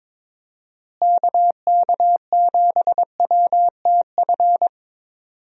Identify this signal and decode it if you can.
Morse code, heard as a keyed tone, about 22 wpm, 700 Hz. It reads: KK7WTF
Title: USA Call Signs - 6 Characters - Repeated 2x - 22wpm